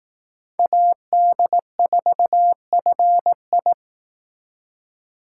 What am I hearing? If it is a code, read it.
Morse code, heard as a keyed tone, about 19 wpm, 700 Hz. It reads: AD4FI
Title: Speed Racing - USA Call Signs 15wpm